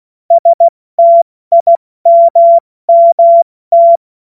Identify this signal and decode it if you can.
Morse code, heard as a keyed tone, about 16 wpm, 680 Hz. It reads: STIMMT
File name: Morsekurs / Witze / Witz 13. September 2022 70 F80